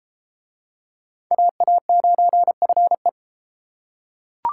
Morse code, heard as a keyed tone, about 31 wpm, 700 Hz. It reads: AA9FE